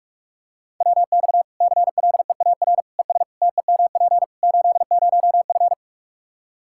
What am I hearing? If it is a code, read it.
Morse code, heard as a keyed tone, about 46 wpm, 700 Hz. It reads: WX CLEAR ES TEMP 80F
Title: K1USN SST QSO Practice 45wpm